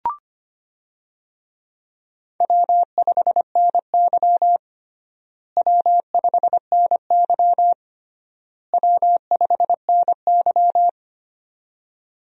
Morse code, heard as a keyed tone, about 25 wpm, 700 Hz. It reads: W5NY W5NY W5NY
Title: USA Call Signs - 4 Characters - Repeated 3x - 25wpm